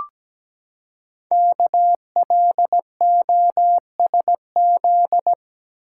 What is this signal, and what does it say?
Morse code, heard as a keyed tone, about 17 wpm, 700 Hz. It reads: KLOSZ
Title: Code Groups - 5 Letters - 17wpm